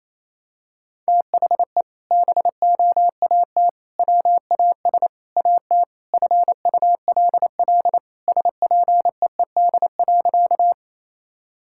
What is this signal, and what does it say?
Morse code, heard as a keyed tone, about 28 wpm, 700 Hz. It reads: THE BOAT WAS AT FULL SPEED.
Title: Sentences from Top 1000 Words - Review Part 5 - 28wpm